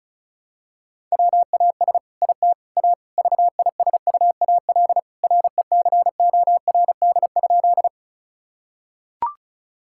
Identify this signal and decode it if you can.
Morse code, heard as a keyed tone, about 36 wpm, 700 Hz. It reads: WAS IT A VISUAL RECORD?
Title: Sentences from Top 1000 Words - Review Part 6 - 35wpm